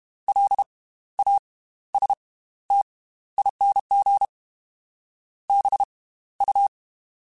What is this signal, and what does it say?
Morse code, heard as a keyed tone, about 32 wpm, spaced as for 15 wpm, 800 Hz. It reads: LASTING BU
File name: Slow ing @30